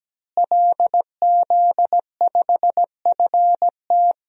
Morse code, heard as a keyed tone, about 17 wpm, 700 Hz. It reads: LZ5FT
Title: DXCC-100 Call Signs - 5 Characters - Repeated 2x - 17wpm